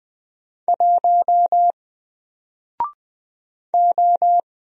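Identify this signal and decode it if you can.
Morse code, heard as a keyed tone, about 20 wpm, 700 Hz. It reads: 1 O
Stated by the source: Single letter and number Morse drill at 20 wpm, from N4LCD.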